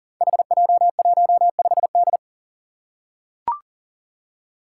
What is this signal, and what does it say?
Morse code, heard as a keyed tone, about 39 wpm, 700 Hz. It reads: HJ15D
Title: Five Challenging Letter-Number Combinations 40wpm